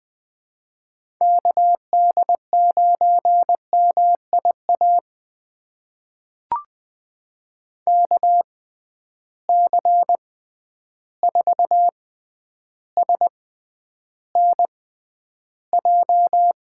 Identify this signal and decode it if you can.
Morse code, heard as a keyed tone, about 20 wpm, spaced as for 10 wpm, 700 Hz. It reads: KD9MIA KC4SNJ
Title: USA Call Signs - 6 Characters - 6x Character Spacing - 20wpm